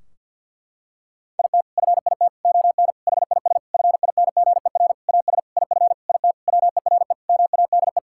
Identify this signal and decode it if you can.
Morse code, heard as a keyed tone, about 50 wpm, 700 Hz. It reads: IT FIT ON HIS FINGER AS IF IT WERE MADE
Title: Sentences from Top 1600 Words 50wpm